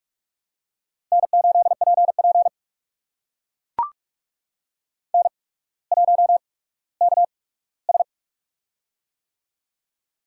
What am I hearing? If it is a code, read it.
Morse code, heard as a keyed tone, about 44 wpm, spaced as for 19 wpm, 690 Hz. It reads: N8PP N1KS
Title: USA Call Signs - 4 Characters - 8x Character Spacing - 45wpm